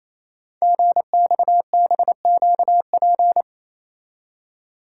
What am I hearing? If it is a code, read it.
Morse code, heard as a keyed tone, about 29 wpm, 700 Hz. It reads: GXBQP
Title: Code Groups - 5 Letters - 28wpm